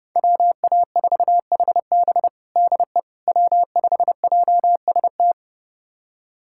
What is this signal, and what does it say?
Morse code, heard as a keyed tone, about 30 wpm, 700 Hz. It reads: WA4HB DE W5JST
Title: USA Callsign DE Callsign - 5 Characters - 30wpm